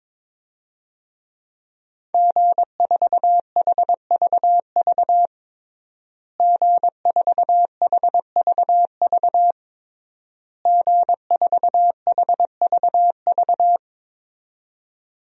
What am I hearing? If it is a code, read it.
Morse code, heard as a keyed tone, about 22 wpm, 700 Hz. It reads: G4HVV G4HVV G4HVV
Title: DXCC-100 Call Signs - 5 Characters - Repeated 3x - 22wpm